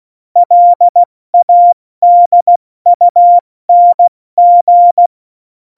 Morse code, heard as a keyed tone, about 16 wpm, 700 Hz. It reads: LADUNG
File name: Morsekurs / Witze / Witz 25. Februar 2020, 70 F80